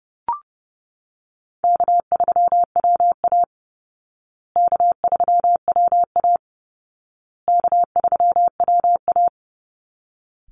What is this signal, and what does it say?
Morse code, heard as a keyed tone, about 30 wpm, 700 Hz. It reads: K3WA K3WA K3WA
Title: Call Signs Repeated 2x - 30wpm